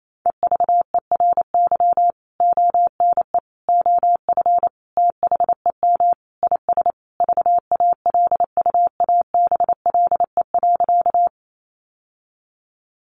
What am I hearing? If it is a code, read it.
Morse code, heard as a keyed tone, about 28 wpm, 700 Hz. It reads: EVERY ONE OF THEM IS VALUABLE.
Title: Sentences from Top 1100 Words 28wpm